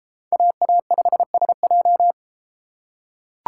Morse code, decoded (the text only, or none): AA5SJ